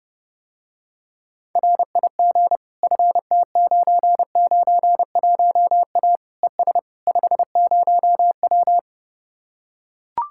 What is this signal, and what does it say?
Morse code, heard as a keyed tone, about 30 wpm, 700 Hz. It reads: RIG FT991A ES 50W